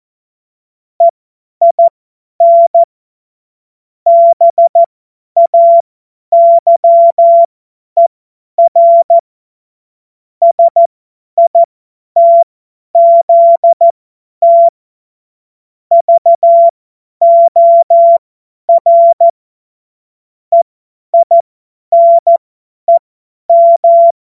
Morse code, heard as a keyed tone, about 14 wpm, spaced as for 10 wpm, 680 Hz. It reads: EIN BAYER SITZT VOR EINEM
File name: Witz-19-August-2025-50-F70